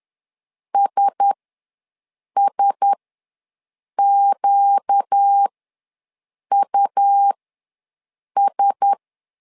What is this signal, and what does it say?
Morse code, heard as a keyed tone, about 10 wpm, spaced as for 6 wpm, 800 Hz. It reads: SSQUS
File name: LEZIONI / 4 Q1S ws 10wpm 2208